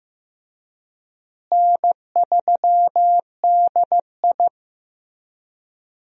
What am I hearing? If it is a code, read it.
Morse code, heard as a keyed tone, about 15 wpm, 700 Hz. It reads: N3DI
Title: USA Call Signs - 4 Characters - Repeated 3x - 15wpm